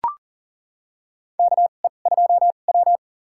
Morse code, heard as a keyed tone, about 40 wpm, spaced as for 29 wpm, 700 Hz. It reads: KE2W
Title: USA Call Signs - 4 Characters - 2x Character Spacing - 40wpm